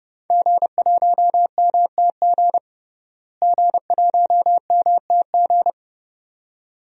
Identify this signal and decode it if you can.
Morse code, heard as a keyed tone, about 30 wpm, 700 Hz. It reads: G1MTG G1MTG
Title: DXCC-100 Call Signs - 5 Characters - Repeated 2x - 30wpm